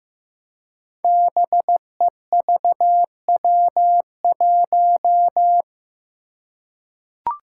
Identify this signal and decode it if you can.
Morse code, heard as a keyed tone, about 15 wpm, 700 Hz. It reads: BEVW1
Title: Five Challenging Letter-Number Combinations 15wpm